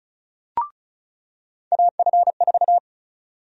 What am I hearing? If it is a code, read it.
Morse code, heard as a keyed tone, about 36 wpm, 700 Hz. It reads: AF4
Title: USA Call Sign Prefixes - 35wpm